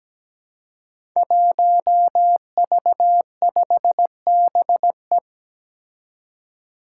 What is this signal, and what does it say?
Morse code, heard as a keyed tone, about 17 wpm, 700 Hz. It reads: 1V5BE
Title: Five Challenging Letter-Number Combinations 17wpm